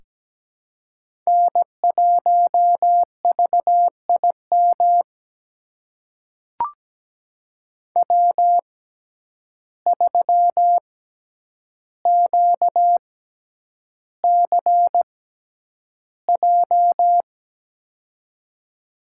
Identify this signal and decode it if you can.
Morse code, heard as a keyed tone, about 17 wpm, spaced as for 9 wpm, 700 Hz. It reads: N1VIM W3QCJ
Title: USA Call Signs - 5 Characters - 6x Character Spacing - 17wpm